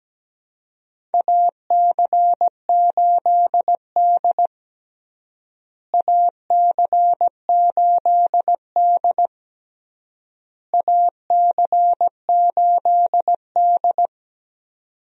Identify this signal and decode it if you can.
Morse code, heard as a keyed tone, about 17 wpm, 700 Hz. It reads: AC8D AC8D AC8D
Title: USA Call Signs - 4 Characters - Repeated 3x - 17wpm